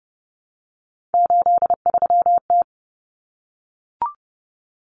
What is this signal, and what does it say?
Morse code, heard as a keyed tone, about 30 wpm, 700 Hz. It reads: 83T
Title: Common Cut Numbers - 3-Digit 30wpm